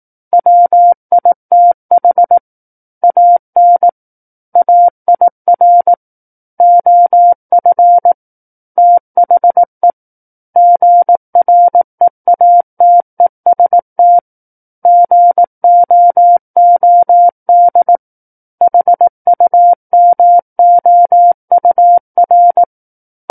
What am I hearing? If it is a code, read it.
Morse code, heard as a keyed tone, about 18 wpm, 700 Hz. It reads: WITH AN AIR OF THE GREATEST GOOD HUMOUR
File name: 18 WPM